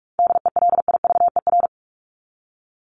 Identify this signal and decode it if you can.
Morse code, heard as a keyed tone, about 44 wpm, 700 Hz. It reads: DELIVER
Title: Top 1751-2000 Words 45wpm